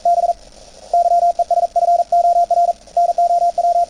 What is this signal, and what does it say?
Morse code, heard as a keyed tone, about 41 wpm, spaced as for 32 wpm, 690 Hz, with radio noise. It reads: BYES FOR NOW
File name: attachments / CW 20m 200Hz